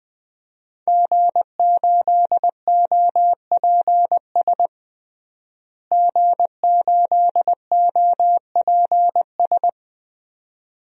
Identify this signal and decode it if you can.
Morse code, heard as a keyed tone, about 20 wpm, 700 Hz. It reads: G8OPS G8OPS
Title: DXCC-100 Call Signs - 5 Characters - Repeated 2x - 20wpm